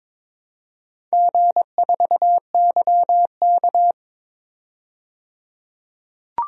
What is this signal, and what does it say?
Morse code, heard as a keyed tone, about 21 wpm, 700 Hz. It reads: G4YK
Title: DXCC-100 Call Signs - 4 Characters - 22wpm